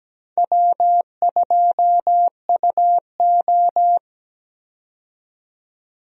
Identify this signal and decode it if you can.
Morse code, heard as a keyed tone, about 17 wpm, 700 Hz. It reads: W2UO
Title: USA Call Signs - 4 Characters - 17wpm